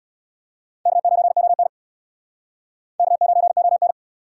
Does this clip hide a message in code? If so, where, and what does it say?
Yes: Morse code from 0.9 to 3.9 s, heard as a keyed tone, about 67 wpm, 700 Hz. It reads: K9ON K9ON